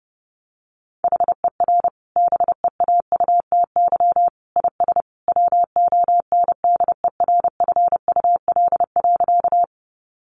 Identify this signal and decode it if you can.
Morse code, heard as a keyed tone, about 28 wpm, 700 Hz. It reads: HER BEAUTY IS WONDERFUL.
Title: Sentences from Top 1100 Words 30wpm